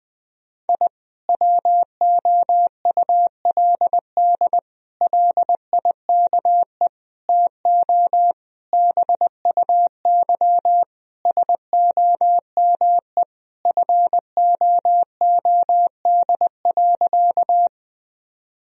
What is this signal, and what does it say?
Morse code, heard as a keyed tone, about 20 wpm, 700 Hz. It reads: I WOULD LIKE TO BUY SOME FOOD.
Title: Sentences from Top 1000 Words - Review Part 4 - 20wpm